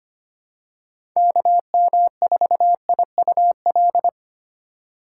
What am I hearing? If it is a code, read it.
Morse code, heard as a keyed tone, about 25 wpm, 700 Hz. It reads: KM4IUL
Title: USA Call Signs - 6 Characters - 3x Character Spacing - 25wpm